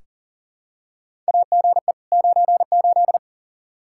Morse code, heard as a keyed tone, about 40 wpm, 700 Hz. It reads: AGE 98
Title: Practice QSO - Exchange 3 - More Info - 40wpm